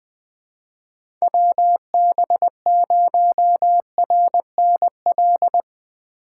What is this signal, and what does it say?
Morse code, heard as a keyed tone, about 20 wpm, 700 Hz. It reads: WB0RNL
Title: USA Call Signs - 6 Characters - 2x Character Spacing - 20wpm